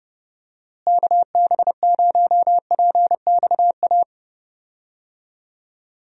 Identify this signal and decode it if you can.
Morse code, heard as a keyed tone, about 32 wpm, 700 Hz. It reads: KB0PXA
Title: USA Call Signs - 6 Characters - Repeated 3x - 30wpm